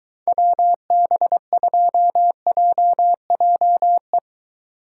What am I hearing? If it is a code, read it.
Morse code, heard as a keyed tone, about 22 wpm, 700 Hz. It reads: WB2JJE